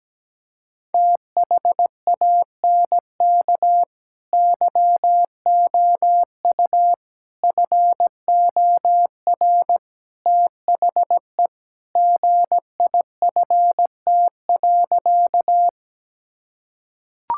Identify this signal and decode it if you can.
Morse code, heard as a keyed tone, about 17 wpm, 700 Hz. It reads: THANK YOU FOR THE GIFT.
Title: Sentences from Top 1000 Words - Review Part 6 - 17wpm